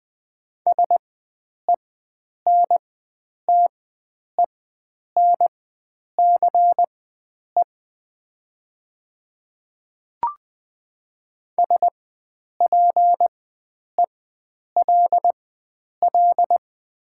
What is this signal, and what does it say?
Morse code, heard as a keyed tone, about 20 wpm, spaced as for 9 wpm, 690 Hz. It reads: SENTENCE SPELL
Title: 4x Character Spacing - Top 500 Words - 20wpm